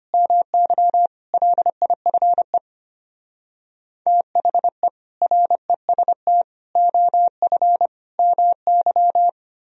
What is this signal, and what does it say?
Morse code, heard as a keyed tone, about 33 wpm, 700 Hz. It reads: MY LIFE THE REST OF MY